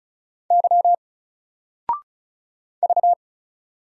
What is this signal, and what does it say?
Morse code, heard as a keyed tone, about 36 wpm, 700 Hz. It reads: Y V
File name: Single-Letter-Number-35wpm